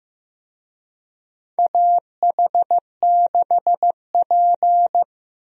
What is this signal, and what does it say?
Morse code, heard as a keyed tone, about 15 wpm, 700 Hz. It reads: AH6P
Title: USA Call Signs - 4 Characters - 15wpm